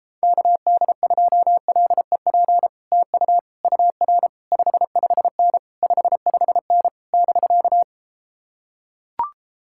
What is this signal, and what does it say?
Morse code, heard as a keyed tone, about 33 wpm, 700 Hz. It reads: KD2LEP TU UR 55N 55N <BK>